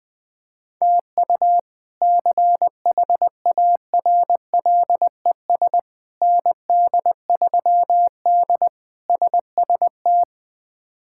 Morse code, heard as a keyed tone, about 20 wpm, 700 Hz. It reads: TU CHARLES ND3D SST
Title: K1USN SST QSO Practice - 20wpm